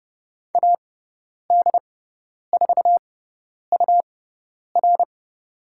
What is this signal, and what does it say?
Morse code, heard as a keyed tone, about 30 wpm, spaced as for 10 wpm, 700 Hz. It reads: AD4UR
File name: Lesson 18g - Letter m - Call Signs 30wpm